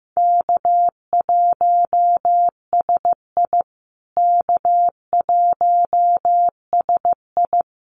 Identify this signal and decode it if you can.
Morse code, heard as a keyed tone, about 15 wpm, 700 Hz. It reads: K1SI K1SI
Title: USA Call Signs Repeated 3x 15wpm